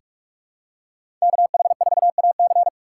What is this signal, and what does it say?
Morse code, heard as a keyed tone, about 46 wpm, 700 Hz. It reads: KH4AC
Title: USA Call Signs - 3x Character Spacing - 45wpm